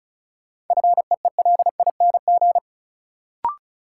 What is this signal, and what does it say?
Morse code, heard as a keyed tone, about 36 wpm, 700 Hz. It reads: FEELING
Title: Sentences from Top 1200 Words - New Word Drill Alt - 35wpm